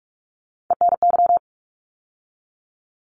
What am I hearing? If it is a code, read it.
Morse code, heard as a keyed tone, about 44 wpm, 700 Hz. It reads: ENY
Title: ARRL Sections 45wpm